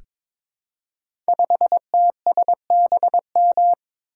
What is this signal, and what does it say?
Morse code, heard as a keyed tone, about 22 wpm, 700 Hz. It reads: 5TSBM
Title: Code Groups - 5 Letter:Numbers - 22wpm